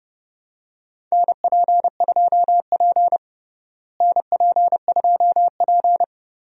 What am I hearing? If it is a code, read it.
Morse code, heard as a keyed tone, about 32 wpm, 700 Hz. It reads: NP2P NP2P
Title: USA Call Signs - 4 Characters - Repeated 3x - 30wpm